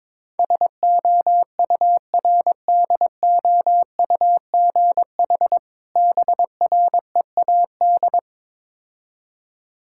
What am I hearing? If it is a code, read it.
Morse code, heard as a keyed tone, about 22 wpm, 690 Hz. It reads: SOURDOUGH BREAD
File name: Difficult Words with Hints - 22wpm